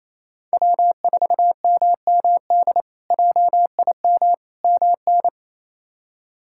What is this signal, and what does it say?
Morse code, heard as a keyed tone, about 27 wpm, 700 Hz. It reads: W4MMD JIM MN
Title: K1USN SST QSO Practice - 28wpm